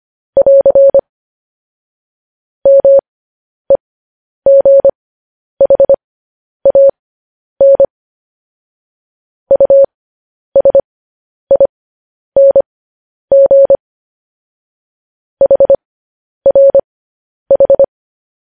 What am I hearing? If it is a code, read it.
Morse code, heard as a keyed tone, about 25 wpm, spaced as for 10 wpm, 550 Hz. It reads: <AR> MEGHAN USING HRH
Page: news headlines in morse code at 10 wpm